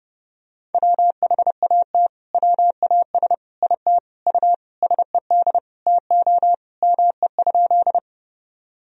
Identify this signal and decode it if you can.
Morse code, heard as a keyed tone, about 30 wpm, 700 Hz. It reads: WHAT WAS IT U SED TO ME?
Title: Sentences from Top 200 Words and Abbreviations 30wpm